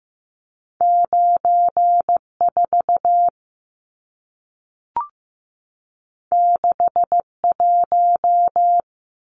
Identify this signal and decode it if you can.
Morse code, heard as a keyed tone, about 15 wpm, 700 Hz. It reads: 94 61
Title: Numbers - 2-Digit 15wpm